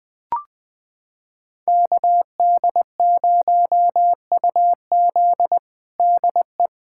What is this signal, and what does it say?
Morse code, heard as a keyed tone, about 20 wpm, 700 Hz. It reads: KD0UZ DE
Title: USA Callsign DE Callsign - 5 Characters - 20wpm